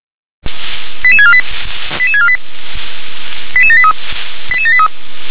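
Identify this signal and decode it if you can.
Five-tone selective call (ZVEI), heard as quick runs of five tones, tones of about 70 ms. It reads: 88648 88648 88783 88783
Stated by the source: ZVEI five-tone call 88740 received by SDR.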